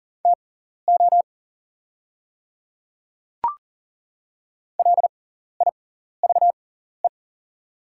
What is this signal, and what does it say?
Morse code, heard as a keyed tone, about 40 wpm, spaced as for 14 wpm, 690 Hz. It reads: TO LIVE
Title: 6x Character Spacing - Top 500 Words - 40wpm